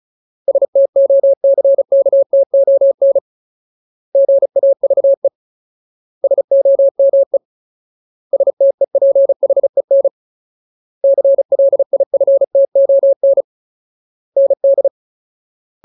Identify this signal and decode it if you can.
Morse code, heard as a keyed tone, about 36 wpm, 550 Hz. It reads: STOCKTON GAVE SOME STEPHEN CLIFTON ND
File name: Ultimate Word List / 35WPM Ultimate